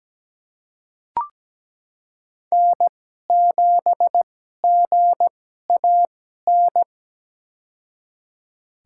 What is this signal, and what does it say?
Morse code, heard as a keyed tone, about 17 wpm, spaced as for 12 wpm, 700 Hz. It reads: N7GAN
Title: USA Call Signs - 5 Characters - 2x Character Spacing - 17wpm